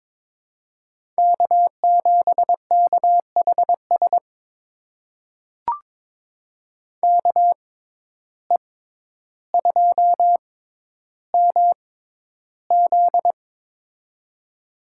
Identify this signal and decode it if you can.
Morse code, heard as a keyed tone, about 21 wpm, spaced as for 11 wpm, 700 Hz. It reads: K7KHS KE2MZ
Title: USA Call Signs - 5 Characters - 6x Character Spacing - 22wpm